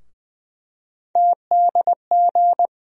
Morse code, heard as a keyed tone, about 20 wpm, 700 Hz. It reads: TDG
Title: USA Call Sign Postfixes - 3 Letters 20wpm